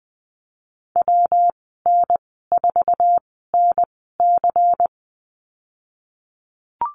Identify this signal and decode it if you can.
Morse code, heard as a keyed tone, about 20 wpm, spaced as for 15 wpm, 700 Hz. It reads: WN4NC
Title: 122 Lesson 20g - Number 3 - Call Signs 20wpm